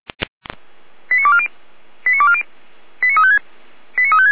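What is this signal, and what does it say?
Five-tone selective call (ZVEI), heard as quick runs of five tones, tones of about 70 ms. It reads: 89240 89240 89356 89356
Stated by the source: ZVEI five-tone selective call received with an SDR.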